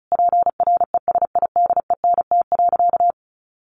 Morse code, heard as a keyed tone, about 34 wpm, 700 Hz. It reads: PRESIDENT.
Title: A Sound Of Thunder 35wpm